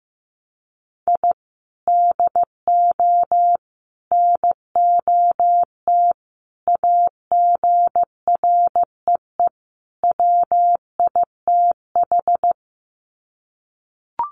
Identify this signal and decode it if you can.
Morse code, heard as a keyed tone, about 15 wpm, 700 Hz. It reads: I DO NOT AGREE WITH
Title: Sets of 5 Words Encore 15wpm